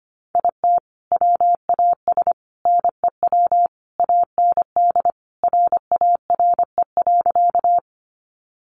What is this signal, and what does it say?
Morse code, heard as a keyed tone, about 25 wpm, 700 Hz. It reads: IT WAS NEW AND RARE.
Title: Sentences from Top 1100 Words 25wpm